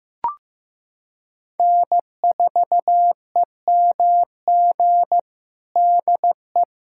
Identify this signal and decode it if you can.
Morse code, heard as a keyed tone, about 15 wpm, 700 Hz. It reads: N4EMG DE
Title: USA Callsign DE Callsign - 5 Characters - 15wpm